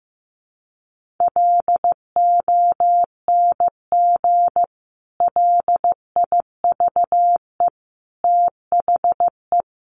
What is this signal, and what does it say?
Morse code, heard as a keyed tone, about 15 wpm, 700 Hz. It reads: LONG LIVE THE
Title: Sentences from Top 1500 Words 15wpm